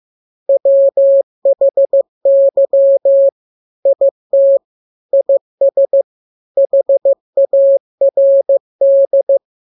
Morse code, heard as a keyed tone, about 15 wpm, 550 Hz. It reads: WHY IT IS HARD